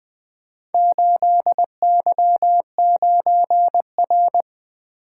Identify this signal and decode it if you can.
Morse code, heard as a keyed tone, about 20 wpm, 700 Hz. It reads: 8Y9R
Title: Four Letter-Number Combinations - 20wpm